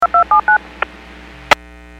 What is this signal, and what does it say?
DTMF, heard as telephone touch tones, tones of about 110 ms. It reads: 2279
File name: WPS / 00239 03-45 WPS 451.25 WPS